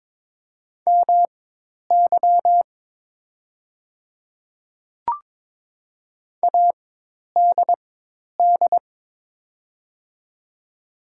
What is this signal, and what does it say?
Morse code, heard as a keyed tone, about 22 wpm, spaced as for 10 wpm, 700 Hz. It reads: MY ADD
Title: 4x Character Spacing - Top 300 Words - 22wpm